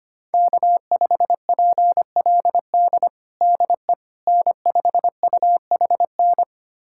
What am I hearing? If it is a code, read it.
Morse code, heard as a keyed tone, about 25 wpm, 700 Hz. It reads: K5PLD DE N5UHN
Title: USA Callsign DE Callsign - 5 Characters - 25wpm